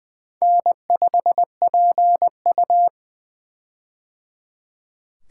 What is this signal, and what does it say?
Morse code, heard as a keyed tone, about 20 wpm, 700 Hz. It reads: N5PU
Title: K1USN SST QSO Practice - 20wpm